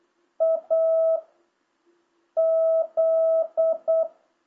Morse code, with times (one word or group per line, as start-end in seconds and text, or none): 0.4-1.2 A
2.4-4.0 Z